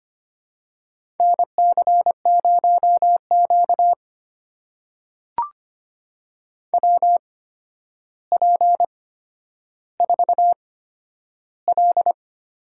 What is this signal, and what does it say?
Morse code, heard as a keyed tone, about 25 wpm, spaced as for 11 wpm, 700 Hz. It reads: NC0Q WP4L
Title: USA Call Signs - 4 Characters - 8x Character Spacing - 25wpm